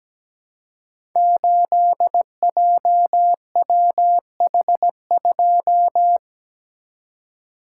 Morse code, heard as a keyed tone, about 17 wpm, 700 Hz. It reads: 8JWH2
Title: Code Groups - 5 Letter:Numbers - 17wpm